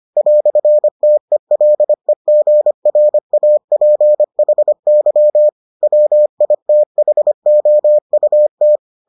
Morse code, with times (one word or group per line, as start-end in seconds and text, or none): 0.2-5.5 "TELEGRAPHY
5.8-8.8 WITHOUT